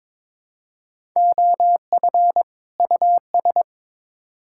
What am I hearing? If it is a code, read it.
Morse code, heard as a keyed tone, about 22 wpm, 700 Hz. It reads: OF US
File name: Build A Phrase - 2-Word Phrases - 22wpm